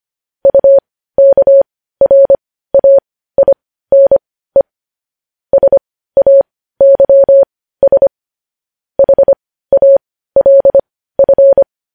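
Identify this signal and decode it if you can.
Morse code, heard as a keyed tone, about 25 wpm, spaced as for 15 wpm, 550 Hz. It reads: UKRAINE SAYS HALF
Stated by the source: Morse code news headlines sent at 15.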